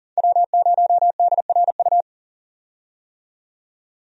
Morse code, heard as a keyed tone, about 40 wpm, 700 Hz. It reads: W0DRU
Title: USA Call Signs - 5 Characters - 40wpm